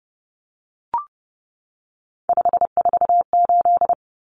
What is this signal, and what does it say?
Morse code, heard as a keyed tone, about 30 wpm, 700 Hz. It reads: 548
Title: Numbers - 3-Digit 30wpm